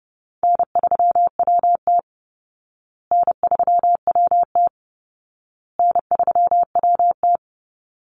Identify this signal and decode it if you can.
Morse code, heard as a keyed tone, about 30 wpm, 700 Hz. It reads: N3WT N3WT N3WT
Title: Easy Call Signs Repeated 3x - 30wpm